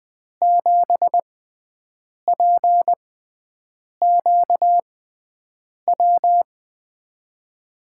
Morse code, heard as a keyed tone, about 20 wpm, spaced as for 7 wpm, 700 Hz. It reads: 7PQW
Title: USA Call Signs - 6 Characters - 6x Character Spacing - 20wpm